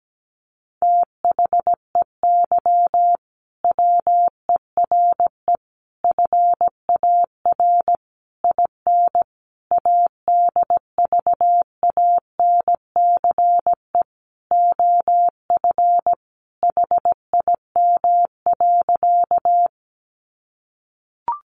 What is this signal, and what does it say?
Morse code, heard as a keyed tone, about 17 wpm, 700 Hz. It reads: THEY WERE FAR IN ADVANCE OF HIM.
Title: Sentences from Top 1400 Words 17wpm